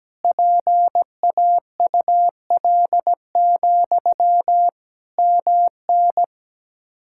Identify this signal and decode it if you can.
Morse code, heard as a keyed tone, about 17 wpm, 700 Hz. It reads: PAUL, MN
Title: K1USN SST QSO Practice 17wpm